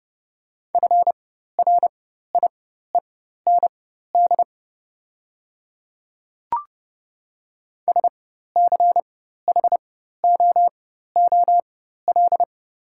Morse code, heard as a keyed tone, about 31 wpm, spaced as for 14 wpm, 700 Hz. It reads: FRIEND SCHOOL
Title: 4x Character Spacing - Top 300 Words - 30wpm